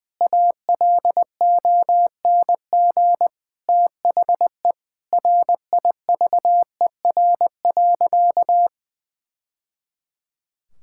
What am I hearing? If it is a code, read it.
Morse code, heard as a keyed tone, about 20 wpm, 700 Hz. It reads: ALONG THE RIVER.
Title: Sentences from Top 1000 Words - Review Part 6 - 20wpm